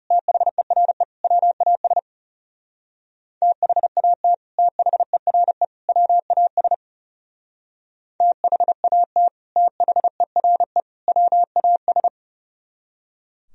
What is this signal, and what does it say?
Morse code, heard as a keyed tone, about 38 wpm, 700 Hz. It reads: THERE WAS THAT THERE WAS THAT THERE WAS